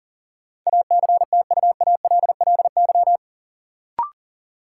Morse code, heard as a keyed tone, about 37 wpm, 700 Hz. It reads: ACTUALLY
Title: Sentences from Top 700 Words - New Word Drill - 40wpm